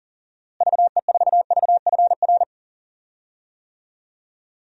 Morse code, heard as a keyed tone, about 39 wpm, 700 Hz. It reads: VE4VFR
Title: DXCC-100 Call Signs - 6 Characters - Repeated 2x - 40wpm